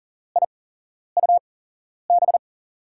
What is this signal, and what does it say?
Morse code, heard as a keyed tone, about 40 wpm, spaced as for 11 wpm, 700 Hz. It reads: IUB
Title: USA Call Signs - 6 Characters - 8x Character Spacing - 40wpm